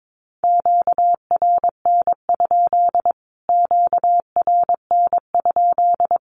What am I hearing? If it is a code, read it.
Morse code, heard as a keyed tone, about 22 wpm, 700 Hz. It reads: QRN? QRN?
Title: Q-Codes Repeated 3x 22wpm